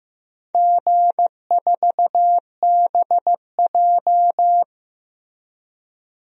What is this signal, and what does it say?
Morse code, heard as a keyed tone, about 15 wpm, 700 Hz. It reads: G4BJ
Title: DXCC-100 Call Signs - 4 Characters - Repeated 2x - 15wpm